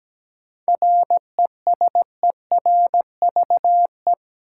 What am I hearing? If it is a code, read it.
Morse code, heard as a keyed tone, about 17 wpm, 700 Hz. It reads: RESERVE
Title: Sentences from Top 1000 Words - Review Part 6 - New Word Drill - 17wpm